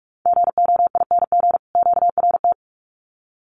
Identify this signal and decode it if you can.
Morse code, heard as a keyed tone, about 44 wpm, 700 Hz. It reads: GOING QRT
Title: Common QSO Phrases 45wpm